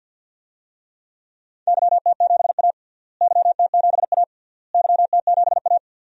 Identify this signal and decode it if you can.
Morse code, heard as a keyed tone, about 52 wpm, 700 Hz. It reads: YT7A YT7A YT7A
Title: DXCC-100 Call Signs - 4 Characters - Repeated 3x - 50wpm